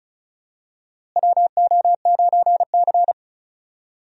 Morse code, heard as a keyed tone, about 34 wpm, 700 Hz. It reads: WO9C
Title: USA Call Signs - 4 Characters - Repeated 3x - 35wpm